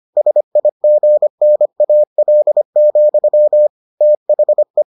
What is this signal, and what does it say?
Morse code, heard as a keyed tone, about 25 wpm, 600 Hz. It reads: SIGNAL, THE